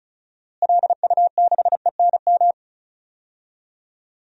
Morse code, heard as a keyed tone, about 36 wpm, 700 Hz. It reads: LU6ENM
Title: DXCC-100 Call Signs - 6 Characters - Repeated 3x - 35wpm